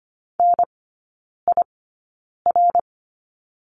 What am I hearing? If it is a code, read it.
Morse code, heard as a keyed tone, about 25 wpm, spaced as for 9 wpm, 700 Hz. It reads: NIR
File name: course / 25:10wpm / 093 Lesson 16e - Letter c - Three Characters 25wpm